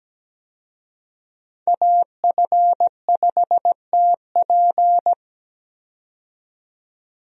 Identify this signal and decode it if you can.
Morse code, heard as a keyed tone, about 17 wpm, 700 Hz. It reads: AF5TP
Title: USA Call Signs - 5 Characters - 17wpm